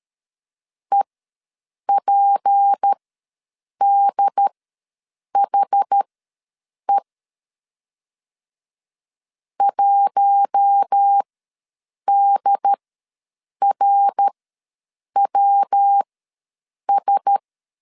Morse code, heard as a keyed tone, about 13 wpm, spaced as for 7 wpm, 800 Hz. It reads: EPDHE 1DRWS
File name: LEZIONI / 6 PKX0 / 6 P ws 12wpm 3117